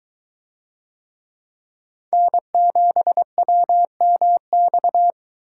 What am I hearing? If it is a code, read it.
Morse code, heard as a keyed tone, about 23 wpm, 700 Hz. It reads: N7WMX